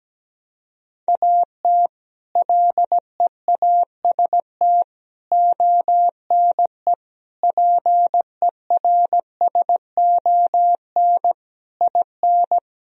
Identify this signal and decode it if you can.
Morse code, heard as a keyed tone, about 17 wpm, 700 Hz. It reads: AT LEAST ONE PERSON IN
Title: Sentences from Top 1000 Words - Review Part 1 - 17wpm